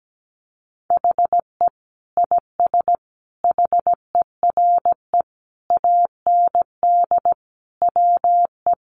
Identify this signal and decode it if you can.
Morse code, heard as a keyed tone, about 17 wpm, 700 Hz. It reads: HE IS HERE AND WE
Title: Sentences from Top 800 Words 17wpm